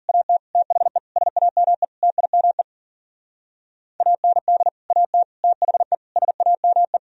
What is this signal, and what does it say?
Morse code, heard as a keyed tone, about 46 wpm, 700 Hz. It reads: AT THE SAME TIME AND AT THE SAME
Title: Speed-Racing - Sets of 5 Words 40wpm